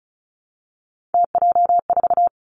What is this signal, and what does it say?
Morse code, heard as a keyed tone, about 36 wpm, 700 Hz. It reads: TJ4
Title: Three Challenging Letter-Number Combinations 35wpm